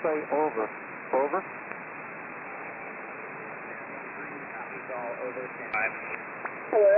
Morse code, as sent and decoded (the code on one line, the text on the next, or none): none